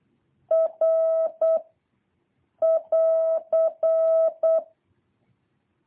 Morse code, heard as a keyed tone, about 8 wpm, 650 Hz. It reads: R <AR>
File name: jh8cbh / 2023cw